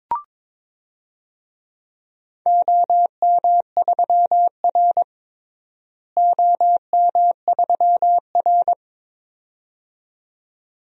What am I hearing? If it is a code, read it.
Morse code, heard as a keyed tone, about 21 wpm, 700 Hz. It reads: OM3R OM3R
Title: DXCC-100 Call Signs - 4 Characters - Repeated 2x - 22wpm